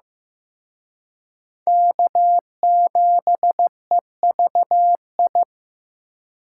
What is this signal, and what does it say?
Morse code, heard as a keyed tone, about 18 wpm, 700 Hz. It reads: K7EVI
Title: Speed Racing - USA 1x3 Call Signs 15wpm